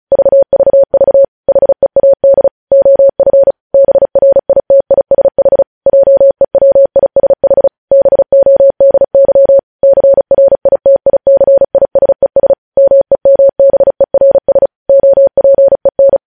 Morse code, heard as a keyed tone, about 36 wpm, 550 Hz. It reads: VVV HEAD OF BRITISH JEWISH BODY CRITICISES MEMBERS OPEN
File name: podcast / News headlines-1x-35x35